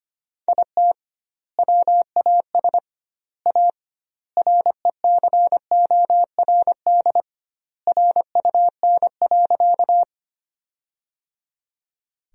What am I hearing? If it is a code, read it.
Morse code, heard as a keyed tone, about 25 wpm, 700 Hz. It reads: IT WAS A RECORD RUN.
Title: 2x Word-Spacing - Sentences from Top 500 Words - 25wpm